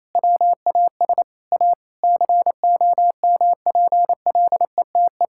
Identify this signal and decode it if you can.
Morse code, heard as a keyed tone, about 27 wpm, 700 Hz. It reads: WAS A COMPLETE
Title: Sentences from Top 1000 Words - Review Part 6 - 28wpm